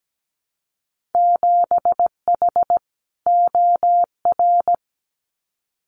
Morse code, heard as a keyed tone, about 17 wpm, 700 Hz. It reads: 7H OR
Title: Winter Field Day Exchanges Encore 17wpm